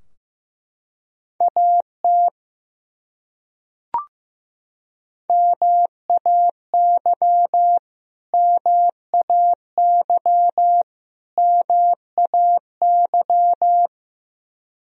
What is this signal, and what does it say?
Morse code, heard as a keyed tone, about 15 wpm, 700 Hz. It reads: AT MAY MAY MAY
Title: Top 100 Words Repeated 15wpm